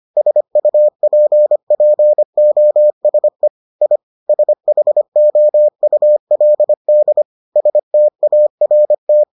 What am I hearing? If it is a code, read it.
Morse code, heard as a keyed tone, about 25 wpm, 600 Hz. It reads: SUPPOSE I SHOULD START